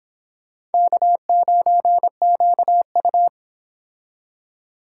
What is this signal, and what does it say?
Morse code, heard as a keyed tone, about 32 wpm, 700 Hz. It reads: K9QU